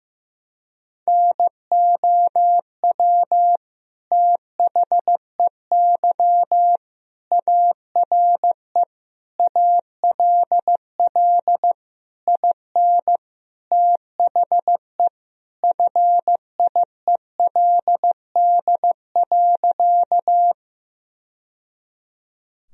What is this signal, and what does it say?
Morse code, heard as a keyed tone, about 16 wpm, 700 Hz. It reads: NOW THEY ARE ALL IN THE FIELD.
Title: Sentences from Top 1000 Words - Review Part 2 - 15wpm